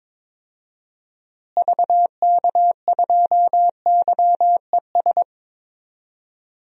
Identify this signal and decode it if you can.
Morse code, heard as a keyed tone, about 22 wpm, 700 Hz. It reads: VK2YES
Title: DXCC-100 Call Signs - 6 Characters - Repeated 2x - 22wpm